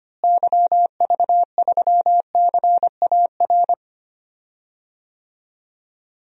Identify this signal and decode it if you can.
Morse code, heard as a keyed tone, about 25 wpm, 700 Hz. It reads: YV3CAR
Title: DXCC-100 Call Signs - 6 Characters - Repeated 2x - 25wpm